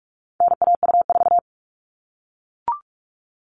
Morse code, heard as a keyed tone, about 44 wpm, 700 Hz. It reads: NAU4
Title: All Cut Numbers - 4-Digit 45wpm